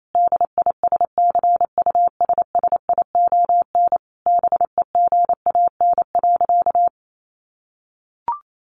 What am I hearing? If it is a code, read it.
Morse code, heard as a keyed tone, about 28 wpm, 700 Hz. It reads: DISCUSSION BEGAN.
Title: Sentences from Top 1100 Words 28wpm